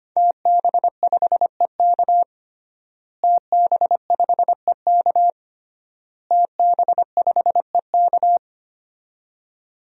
Morse code, heard as a keyed tone, about 25 wpm, 700 Hz. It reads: TB5EK TB5EK TB5EK
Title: DXCC-100 Call Signs - 5 Characters - Repeated 3x - 25wpm